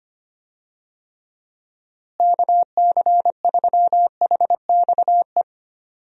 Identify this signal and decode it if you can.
Morse code, heard as a keyed tone, about 25 wpm, 700 Hz. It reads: KC3HXE
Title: USA Call Signs - 6 Characters - Repeated 3x - 25wpm